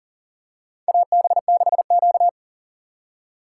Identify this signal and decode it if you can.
Morse code, heard as a keyed tone, about 40 wpm, 700 Hz. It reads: AB6Q
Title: USA Call Signs - 4x Character Spacing - 40wpm